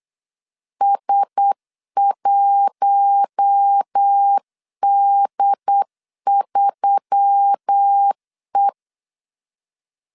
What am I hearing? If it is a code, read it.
Morse code, heard as a keyed tone, about 8 wpm, 800 Hz. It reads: S1D3E
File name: CW / DET / 1 6 1500C S C 8wpm 5219